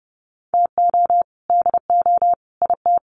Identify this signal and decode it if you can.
Morse code, heard as a keyed tone, about 30 wpm, 700 Hz. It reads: TO DO IT
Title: Sets of 3 Words 30wpm